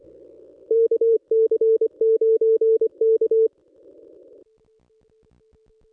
Morse code, heard as a keyed tone, about 25 wpm, 450 Hz. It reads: KC9K
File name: contests / ss-cw-k5tr-op-n5rz -1200